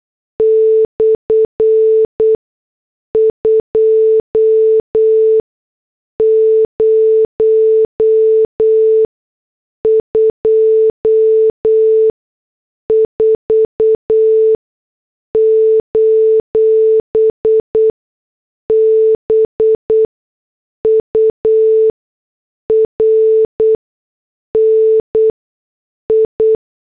Morse code, heard as a keyed tone, about 8 wpm, 440 Hz. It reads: / 2 0 2 4 : B U R N I
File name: morse / headlines